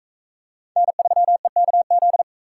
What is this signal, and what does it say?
Morse code, heard as a keyed tone, about 42 wpm, 700 Hz. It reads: N3EKZ